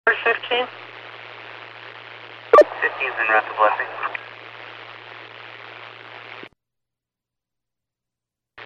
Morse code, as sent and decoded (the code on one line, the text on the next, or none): none